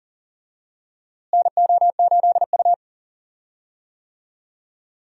Morse code, heard as a keyed tone, about 40 wpm, 700 Hz. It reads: NO8U
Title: USA Call Signs - 4 Characters - Repeated 3x - 40wpm